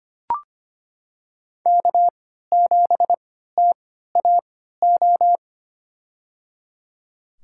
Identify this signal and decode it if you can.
Morse code, heard as a keyed tone, about 25 wpm, spaced as for 14 wpm, 700 Hz. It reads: K7TAO
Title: USA Call Signs - 5 Characters - 3x Character Spacing - 25wpm